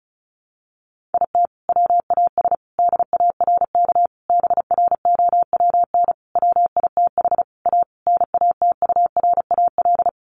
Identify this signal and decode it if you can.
Morse code, heard as a keyed tone, about 34 wpm, 700 Hz. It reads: IT WAS DARK BROWN WITH A NATURAL
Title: Sentences from Top 1400 Words 35wpm